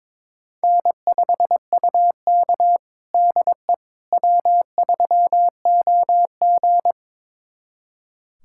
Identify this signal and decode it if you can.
Morse code, heard as a keyed tone, about 22 wpm, 700 Hz. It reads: N5UK DE W3OG
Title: USA Callsign DE Callsign - 4 Characters - 22wpm